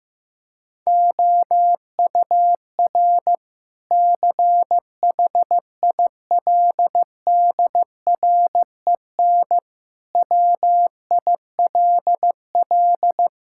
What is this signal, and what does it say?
Morse code, heard as a keyed tone, about 15 wpm, 700 Hz. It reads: OUR CHILDREN WILL
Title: Sentences from Top 1000 Words - Review Part 5 - 15wpm